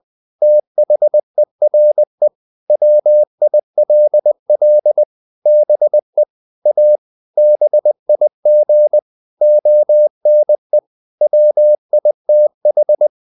Morse code, heard as a keyed tone, about 20 wpm, 600 Hz. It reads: THERE WILL BE A BIG ONE WITH
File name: the old man and the sea / the old man and the sea part 1 20wpm